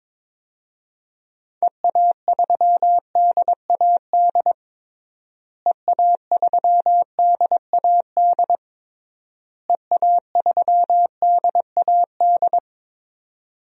Morse code, heard as a keyed tone, about 22 wpm, 700 Hz. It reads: EA3DAD EA3DAD EA3DAD
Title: DXCC-100 Call Signs - 6 Characters - Repeated 3x - 22wpm